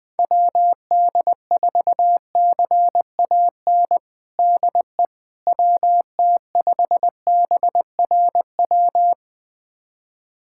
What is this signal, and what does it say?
Morse code, heard as a keyed tone, about 20 wpm, 700 Hz. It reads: WD4CAN DE WT5BRW
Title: USA Callsign DE Callsign 20wpm